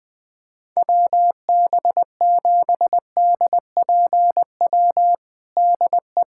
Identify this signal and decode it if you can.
Morse code, heard as a keyed tone, about 20 wpm, 700 Hz. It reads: WB7DPW DE
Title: USA Callsign DE Callsign - 6 Characters - 20wpm